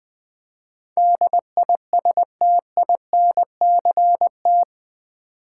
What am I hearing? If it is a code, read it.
Morse code, heard as a keyed tone, about 20 wpm, 700 Hz. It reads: DISTINCT
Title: Build A Word - 4 Letters At A Time - 11-Letter Words - 20wpm